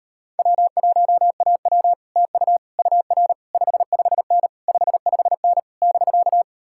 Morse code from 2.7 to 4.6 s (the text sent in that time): UR 55N